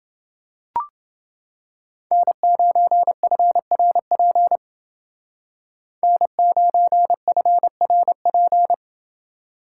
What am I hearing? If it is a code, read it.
Morse code, heard as a keyed tone, about 28 wpm, 700 Hz. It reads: N9FRP N9FRP